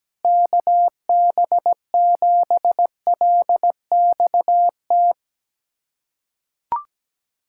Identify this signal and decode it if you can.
Morse code, heard as a keyed tone, about 17 wpm, 700 Hz. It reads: KB7LXT
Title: USA Callsign DE Callsign - 6 Characters - 17wpm